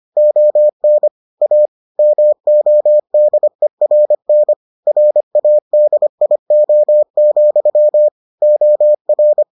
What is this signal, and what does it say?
Morse code, heard as a keyed tone, about 25 wpm, 600 Hz. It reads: ON A MODERN RADIO, OR